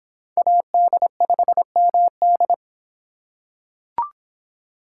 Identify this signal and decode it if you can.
Morse code, heard as a keyed tone, about 25 wpm, 700 Hz. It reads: AD5MD